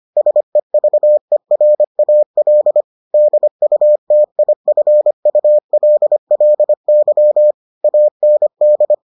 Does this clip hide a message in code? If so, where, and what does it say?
Yes: Morse code from 0.2 to 8.9 s, heard as a keyed tone, about 25 wpm, 600 Hz. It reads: SEVERAL DUTIFULLY AND